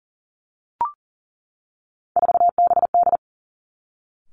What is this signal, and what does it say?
Morse code, heard as a keyed tone, about 42 wpm, 700 Hz. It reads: 4BD
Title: Three Challenging Letter-Number Combinations 40wpm